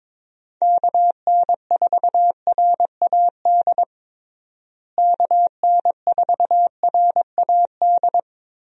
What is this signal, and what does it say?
Morse code, heard as a keyed tone, about 22 wpm, 700 Hz. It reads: KN4RAD KN4RAD
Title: USA Call Signs - 6 Characters - Repeated 2x - 22wpm